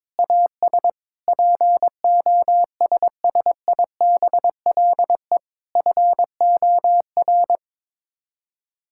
Morse code, heard as a keyed tone, about 22 wpm, 700 Hz. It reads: AS POSSIBLE FOR
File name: practice / 22wpm / Sets of 8 Words - 22wpm